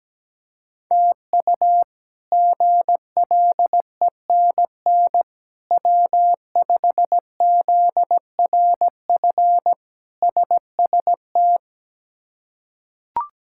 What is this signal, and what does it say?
Morse code, heard as a keyed tone, about 17 wpm, 700 Hz. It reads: TU GLENN W5ZRF SST
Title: K1USN SST QSO Practice - 17wpm